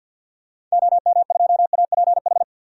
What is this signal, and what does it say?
Morse code, heard as a keyed tone, about 50 wpm, 700 Hz. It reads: OM2APH
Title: DXCC-100 Call Signs - 6 Characters - 50wpm